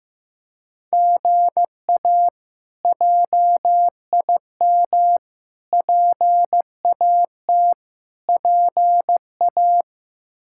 Morse code, heard as a keyed tone, about 15 wpm, 700 Hz. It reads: GA JIM PAT PA
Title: K1USN SST QSO Practice - 15wpm